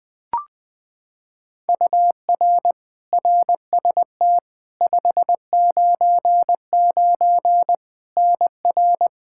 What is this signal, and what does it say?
Morse code, heard as a keyed tone, about 20 wpm, 700 Hz. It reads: UR RST 599 NR
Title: Practice QSO - Exchange 1 - Beginning - 20wpm